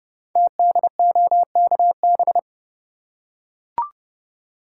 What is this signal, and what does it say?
Morse code, heard as a keyed tone, about 30 wpm, 700 Hz. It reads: TDOKB
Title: Code Groups - 5 Letters - 30wpm